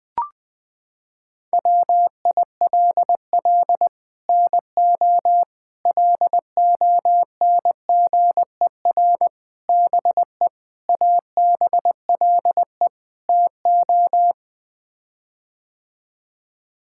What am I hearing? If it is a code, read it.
Morse code, heard as a keyed tone, about 21 wpm, 700 Hz. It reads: WILL NO LONGER BE ABLE TO
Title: Sets of 6 Words - 20wpm